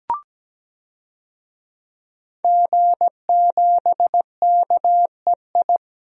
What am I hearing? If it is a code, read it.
Morse code, heard as a keyed tone, about 17 wpm, 700 Hz. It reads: G7KEI
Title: DXCC-100 Call Signs - 5 Characters - Repeated 3x - 17wpm